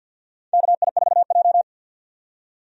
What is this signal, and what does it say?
Morse code, heard as a keyed tone, about 48 wpm, 700 Hz. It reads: KI4J
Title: USA Call Signs - 4 Characters - 50wpm